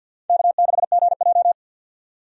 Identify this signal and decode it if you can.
Morse code, heard as a keyed tone, about 49 wpm, 700 Hz. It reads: K6GJ
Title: USA Call Signs - 4 Characters - 50wpm